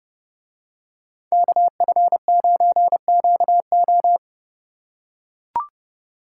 Morse code, heard as a keyed tone, about 29 wpm, 700 Hz. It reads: KF9QO